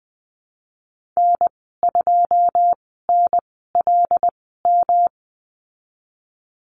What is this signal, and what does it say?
Morse code, heard as a keyed tone, about 20 wpm, spaced as for 14 wpm, 700 Hz. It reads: N2NLM
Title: 128 Lesson 21g - Number 6 - Call Signs 20wpm